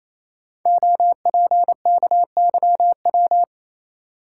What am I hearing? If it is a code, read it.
Morse code, heard as a keyed tone, about 28 wpm, 700 Hz. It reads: OPKYW
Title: Code Groups - 5 Letter:Numbers - 28wpm